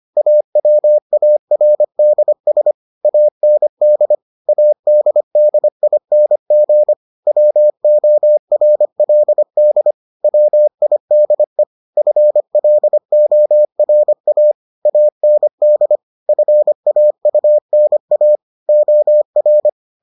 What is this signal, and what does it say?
Morse code, heard as a keyed tone, about 25 wpm, 600 Hz. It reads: AWARDS AND ADDING WORLD WIDE FLORA AND FAUNA OR